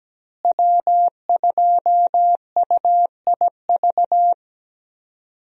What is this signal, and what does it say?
Morse code, heard as a keyed tone, about 17 wpm, 700 Hz. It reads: W2UIV
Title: USA Call Signs - 5 Characters - Repeated 3x - 17wpm